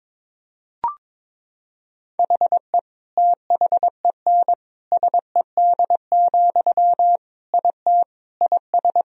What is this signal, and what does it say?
Morse code, heard as a keyed tone, about 22 wpm, 700 Hz. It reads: HE THEN SED, IT IS
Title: Sentences from Top 300 Words and Abbreviations 22wpm